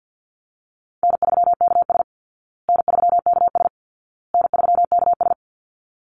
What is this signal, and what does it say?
Morse code, heard as a keyed tone, about 49 wpm, 700 Hz. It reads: N3KS N3KS N3KS
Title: Easy Call Signs Repeated 3x - 50wpm